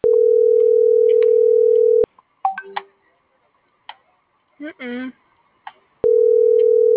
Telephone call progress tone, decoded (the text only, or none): ringback tone